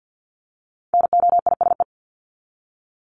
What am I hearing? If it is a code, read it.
Morse code, heard as a keyed tone, about 48 wpm, 700 Hz. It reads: NOISE
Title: Top 1501-1750 Words 50wpm